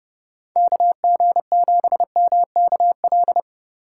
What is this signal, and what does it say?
Morse code, heard as a keyed tone, about 29 wpm, 700 Hz. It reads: KG7MKL